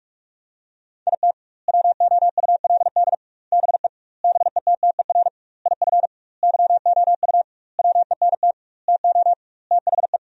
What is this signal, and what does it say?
Morse code, heard as a keyed tone, about 46 wpm, 700 Hz. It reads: IT WOULD BE BETTER IF YOU WENT TO THE
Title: 2x Word-Spacing - Sentences from Top 400 Words - 45wpm